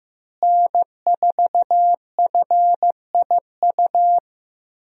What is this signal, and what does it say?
Morse code, heard as a keyed tone, about 15 wpm, 700 Hz. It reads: N4FIU
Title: USA Call Signs - 5 Characters - Repeated 2x - 15wpm